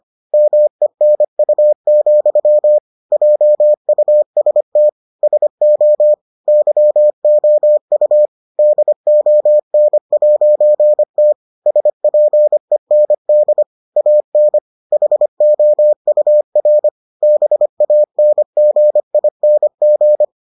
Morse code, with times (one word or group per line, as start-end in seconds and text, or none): 0.3-2.8 MENU,
3.1-4.9 JUST
5.2-6.1 SO
6.5-8.3 YOU
8.6-11.3 DON'T
11.7-13.6 SPEND
14.0-14.6 AN
14.9-16.9 HOUR
17.2-20.2 BANGING